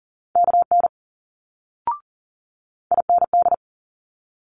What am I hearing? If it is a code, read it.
Morse code, heard as a keyed tone, about 41 wpm, 700 Hz. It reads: KN IND
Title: Multi-letter Phonemes Quick Fire 40wpm